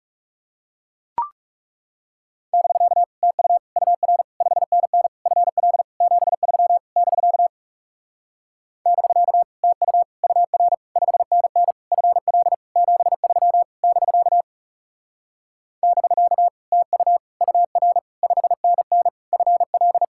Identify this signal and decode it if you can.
Morse code, heard as a keyed tone, about 40 wpm, 700 Hz. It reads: <BK> TU UR 5NN FL 73 <BK> <BK> TU UR 5NN FL 73 <BK> <BK> TU UR 5NN FL